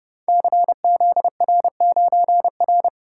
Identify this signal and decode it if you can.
Morse code, heard as a keyed tone, about 30 wpm, 700 Hz. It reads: CZR9R
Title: Code Groups - 5 Letter:Numbers - 30wpm